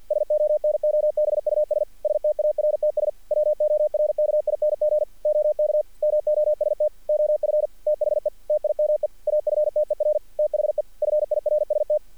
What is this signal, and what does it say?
Morse code, heard as a keyed tone, about 48 wpm, 600 Hz. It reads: SONOBUS STARTS WORKING OK MOST OF THE TIME AFTER THE FIRST
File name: archived / CRSnet